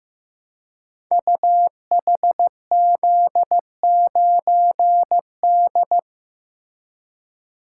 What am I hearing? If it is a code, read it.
Morse code, heard as a keyed tone, about 15 wpm, 700 Hz. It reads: UHZ9D
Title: Code Groups - 5 Letter:Numbers - 15wpm